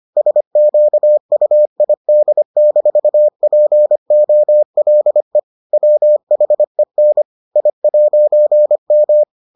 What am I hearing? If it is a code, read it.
Morse code, heard as a keyed tone, about 25 wpm, 600 Hz. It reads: SQUID-POLE WHEN I'M